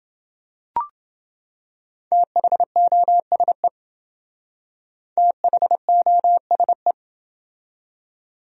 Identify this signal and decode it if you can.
Morse code, heard as a keyed tone, about 27 wpm, 700 Hz. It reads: THOSE THOSE